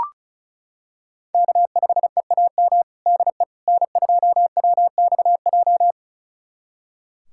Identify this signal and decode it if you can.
Morse code, heard as a keyed tone, about 36 wpm, 700 Hz. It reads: K5EAM DE N2WXJ
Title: USA Callsign DE Callsign - 5 Characters - 35wpm